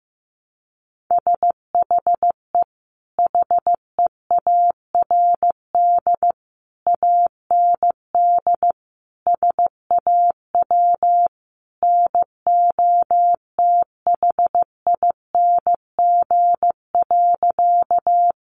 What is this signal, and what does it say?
Morse code, heard as a keyed tone, about 15 wpm, 700 Hz. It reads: SHE HEARD AND SAW NOTHING.
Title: Sentences from Top 500 Words 15wpm